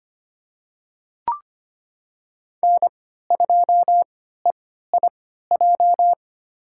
Morse code, heard as a keyed tone, about 25 wpm, spaced as for 14 wpm, 700 Hz. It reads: N2EIJ
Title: USA Call Signs - 3x Character Spacing - 25wpm